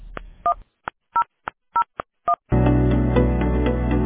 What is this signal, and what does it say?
DTMF, heard as telephone touch tones, tones of about 90 ms. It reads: 1001